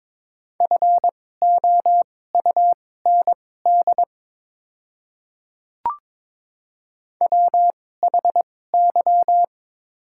Morse code, heard as a keyed tone, about 22 wpm, spaced as for 16 wpm, 690 Hz. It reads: FOUND WHY
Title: 2x Character Spacing - Top 500 Words - 22wpm